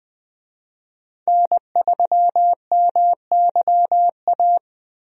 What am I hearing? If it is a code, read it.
Morse code, heard as a keyed tone, about 21 wpm, 700 Hz. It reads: N3MYA
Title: USA Call Signs - 5 Characters - Repeated 2x - 20wpm